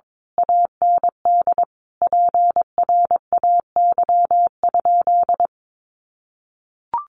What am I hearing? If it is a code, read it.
Morse code, heard as a keyed tone, about 22 wpm, 700 Hz. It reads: AND PRAY?
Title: A Sound Of Thunder 22wpm